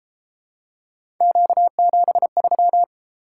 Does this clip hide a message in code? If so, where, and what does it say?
Yes: Morse code from 1.2 to 2.8 s, heard as a keyed tone, about 31 wpm, 700 Hz. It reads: Q73